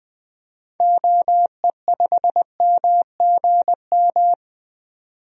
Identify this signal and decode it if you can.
Morse code, heard as a keyed tone, about 20 wpm, 700 Hz. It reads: OE5MGM
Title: DXCC-100 Call Signs - 6 Characters - 20wpm